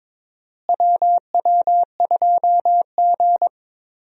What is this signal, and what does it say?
Morse code, heard as a keyed tone, about 22 wpm, 700 Hz. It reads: WW2G
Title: USA Call Signs - 4 Characters - Repeated 2x - 22wpm